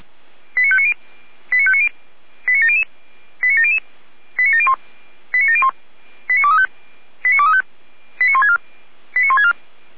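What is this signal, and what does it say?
Five-tone selective call (ZVEI), heard as quick runs of five tones, tones of about 70 ms. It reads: 89590 89700 89700 89701 89701 89235 89235 89164 89164
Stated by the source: SDR-received ZVEI five-tone selective call, code 89120.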